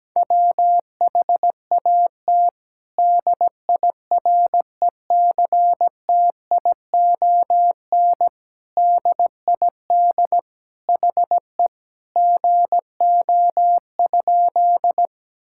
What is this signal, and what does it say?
Morse code, heard as a keyed tone, about 17 wpm, 700 Hz. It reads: WHAT DIRECTION DID HE GO?
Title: Sentences from Top 1000 Words - Review Part 5 - 17wpm